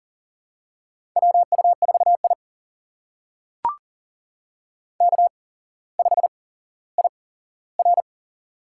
Morse code, heard as a keyed tone, about 43 wpm, spaced as for 17 wpm, 700 Hz. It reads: WU4I K5IR
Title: USA Call Signs - 4 Characters - 8x Character Spacing - 40wpm